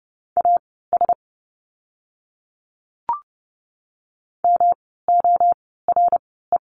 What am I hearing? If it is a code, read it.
Morse code, heard as a keyed tone, about 30 wpm, spaced as for 17 wpm, 700 Hz. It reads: AS MORE
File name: course / 30:10wpm / 115 Lesson 19f - Letter w - Words 30wpm